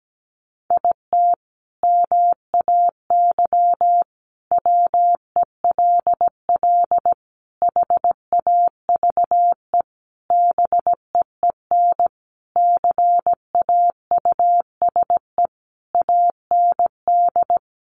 Morse code, heard as a keyed tone, about 17 wpm, 700 Hz. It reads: IT MAY WELL HAVE BEEN CAUSE AND
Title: Sentences from Top 1000 Words 17wpm